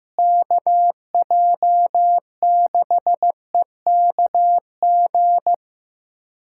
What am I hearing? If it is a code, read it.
Morse code, heard as a keyed tone, about 15 wpm, 700 Hz. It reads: KJ6EKG
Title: USA Callsign DE Callsign - 6 Characters - 15wpm